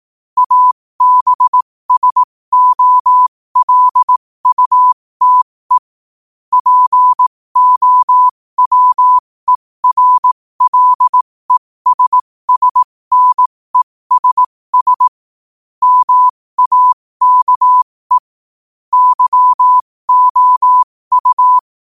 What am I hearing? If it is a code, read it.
Morse code, heard as a keyed tone, about 18 wpm, spaced as for 15 wpm, 1000 Hz. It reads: ABSOLUTE POWERLESSNESS MAKE YOU
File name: qotdfiles / 15wpm